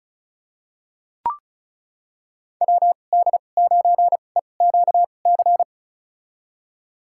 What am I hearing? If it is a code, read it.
Morse code, heard as a keyed tone, about 34 wpm, spaced as for 25 wpm, 700 Hz. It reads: WD9EQC
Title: USA Call Signs - 6 Characters - 2x Character Spacing - 35wpm